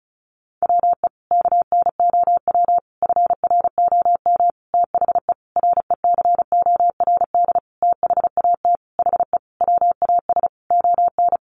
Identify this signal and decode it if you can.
Morse code, heard as a keyed tone, about 34 wpm, 700 Hz. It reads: WE KNOW FROM THE RECORD THAT HE WAS ON
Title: Sentences from Top 500 Words 35wpm